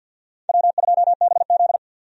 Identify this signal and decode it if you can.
Morse code, heard as a keyed tone, about 53 wpm, 700 Hz. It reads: W2BZ
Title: USA Call Signs - 4 Characters - 50wpm